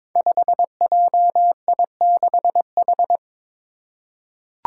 Morse code, heard as a keyed tone, about 22 wpm, 700 Hz. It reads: 5JI6H
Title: Five Challenging Letter-Number Combinations 22wpm